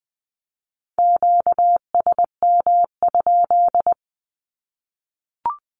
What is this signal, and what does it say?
Morse code, heard as a keyed tone, about 20 wpm, 700 Hz. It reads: QSM?
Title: Q-Codes 20wpm